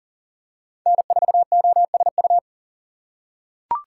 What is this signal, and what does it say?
Morse code, heard as a keyed tone, about 40 wpm, 700 Hz. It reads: N4OSU
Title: USA Call Signs - 5 Characters - 6x Character Spacing - 40wpm